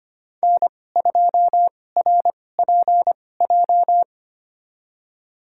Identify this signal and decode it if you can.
Morse code, heard as a keyed tone, about 25 wpm, spaced as for 18 wpm, 700 Hz. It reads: N2RPJ
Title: USA Call Signs - 5 Characters - 2x Character Spacing - 25wpm